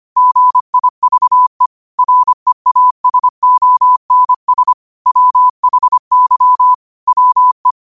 Morse code, heard as a keyed tone, about 25 wpm, 1000 Hz. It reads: GIVE REASONS WHY WE
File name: qotdfiles / 25wpm